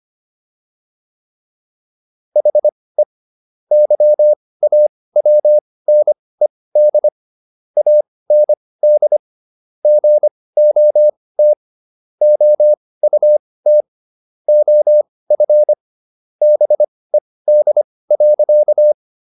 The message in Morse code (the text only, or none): HE YAWNED AND GOT OUT OF BED.